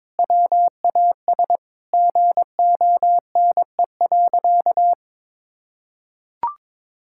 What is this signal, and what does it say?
Morse code, heard as a keyed tone, about 22 wpm, 700 Hz. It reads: WAS GONE.
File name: practice / 22wpm / Sentences with 3-Word Phrases - 22wpm